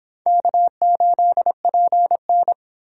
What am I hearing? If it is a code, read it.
Morse code, heard as a keyed tone, about 25 wpm, 700 Hz. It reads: K8PN